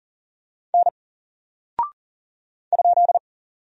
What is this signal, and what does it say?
Morse code, heard as a keyed tone, about 40 wpm, 700 Hz. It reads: N ?
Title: Single Character (Letter:Number:Punctuation:Prosign) 40wpm